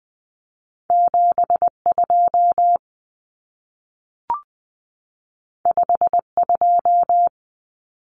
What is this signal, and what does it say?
Morse code, heard as a keyed tone, about 20 wpm, 700 Hz. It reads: 72 52
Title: Common Cut Numbers - 2-Digit 20wpm